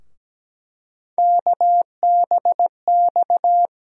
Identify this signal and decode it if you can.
Morse code, heard as a keyed tone, about 17 wpm, 700 Hz. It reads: KBX
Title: USA Call Sign Postfixes - 3 Letters 17wpm